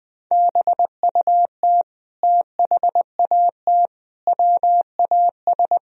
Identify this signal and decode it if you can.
Morse code, heard as a keyed tone, about 20 wpm, 700 Hz. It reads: BUT THAT WAS
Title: Sentences from Top 1000 Words - Review Part 1 - 20wpm